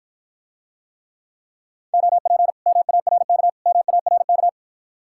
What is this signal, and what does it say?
Morse code, heard as a keyed tone, about 53 wpm, 700 Hz. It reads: OP MARK MARK